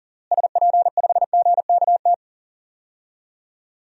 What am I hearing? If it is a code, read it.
Morse code, heard as a keyed tone, about 37 wpm, 700 Hz. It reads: SP5GKT
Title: DXCC-100 Call Signs - 6 Characters - Repeated 3x - 40wpm